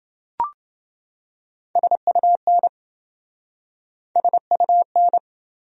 Morse code, heard as a keyed tone, about 27 wpm, 700 Hz. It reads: SUN SUN